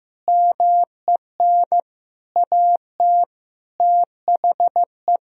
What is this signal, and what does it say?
Morse code, heard as a keyed tone, about 15 wpm, 700 Hz. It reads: MEN AT THE
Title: Sentences from Top 1000 Words - Review Part 4 - 15wpm